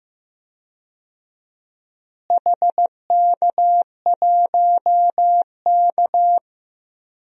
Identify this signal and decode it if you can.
Morse code, heard as a keyed tone, about 15 wpm, 700 Hz. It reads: HK1K
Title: DXCC-100 Call Signs - 4 Characters - Repeated 3x - 15wpm